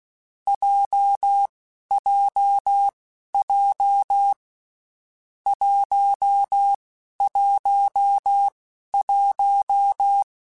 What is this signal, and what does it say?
Morse code, heard as a keyed tone, about 16 wpm, spaced as for 11 wpm, 800 Hz. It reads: JJJ 111
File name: Basic Sets / Basic Letter Groups Collective7 / JZ17@10